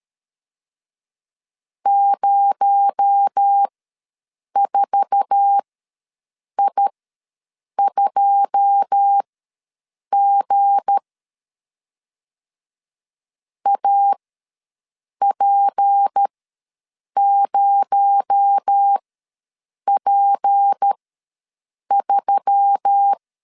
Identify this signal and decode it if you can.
Morse code, heard as a keyed tone, about 13 wpm, spaced as for 7 wpm, 800 Hz. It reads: 04I2G AP0P3
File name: LEZIONI / 9 2 ws 12wpm 3406